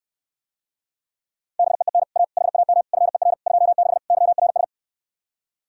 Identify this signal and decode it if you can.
Morse code, heard as a keyed tone, about 69 wpm, 700 Hz. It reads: BEEN A HAM FER 26 YRS